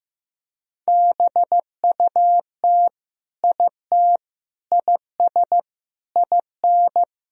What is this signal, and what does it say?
Morse code, heard as a keyed tone, about 16 wpm, 700 Hz. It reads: BUT IT IS IN
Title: Sentences from Top 1000 Words - Review Part 3 - 15wpm